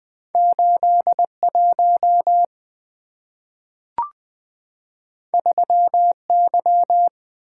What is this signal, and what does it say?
Morse code, heard as a keyed tone, about 20 wpm, 700 Hz. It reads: 81 3Y
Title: Two Letter-Number Combinations - 20wpm